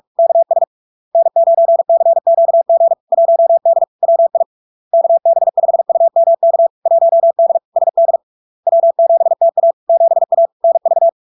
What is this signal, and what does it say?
Morse code, heard as a keyed tone, about 46 wpm, 670 Hz. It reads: KS N9CQG 1D WI KB5UMK 1D SD W7TA 7A NV
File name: LARK-FD-2022-Log-in-CW-45wpm-670-Hz